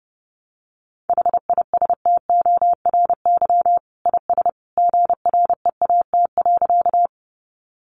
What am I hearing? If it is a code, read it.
Morse code, heard as a keyed tone, about 30 wpm, 700 Hz. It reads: HISTORY IS GREAT.
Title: Sentences from Top 600 Words 30wpm